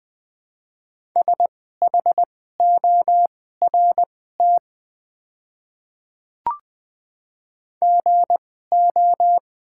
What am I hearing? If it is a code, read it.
Morse code, heard as a keyed tone, about 20 wpm, spaced as for 14 wpm, 690 Hz. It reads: SHORT GO
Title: 2x Character Spacing - Top 500 Words - 20wpm